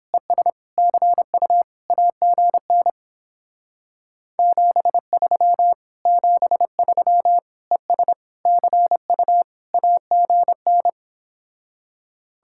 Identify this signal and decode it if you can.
Morse code, heard as a keyed tone, about 29 wpm, 700 Hz. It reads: ES CU AGN 73 73 ES CU AGN